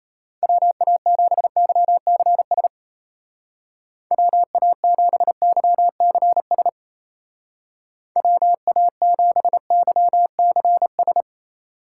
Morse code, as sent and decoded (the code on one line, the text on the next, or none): .-- .- --... -.-- -.-. ... / .-- .- --... -.-- -.-. ... / .-- .- --... -.-- -.-. ...
WA7YCS WA7YCS WA7YCS